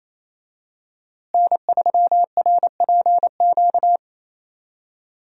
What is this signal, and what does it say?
Morse code, heard as a keyed tone, about 28 wpm, 700 Hz. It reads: N3RPQ
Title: USA Call Signs - 5 Characters - Repeated 3x - 28wpm